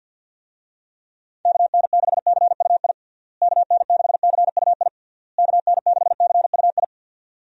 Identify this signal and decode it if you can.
Morse code, heard as a keyed tone, about 47 wpm, 700 Hz. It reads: KN6CUI KN6CUI KN6CUI
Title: USA Call Signs - 6 Characters - Repeated 3x - 50wpm